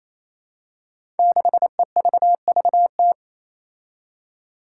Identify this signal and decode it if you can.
Morse code, heard as a keyed tone, about 28 wpm, 700 Hz. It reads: 6EVVT
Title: Five Challenging Letter-Number Combinations 28wpm